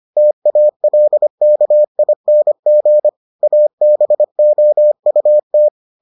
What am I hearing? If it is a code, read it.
Morse code, heard as a keyed tone, about 25 wpm, 600 Hz. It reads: TALKING ABOUT